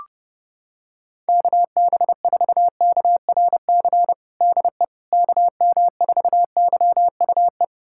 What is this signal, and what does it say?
Morse code, heard as a keyed tone, about 28 wpm, 700 Hz. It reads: KB4KRC DE KM4YUE
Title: USA Callsign DE Callsign - 6 Characters - 30wpm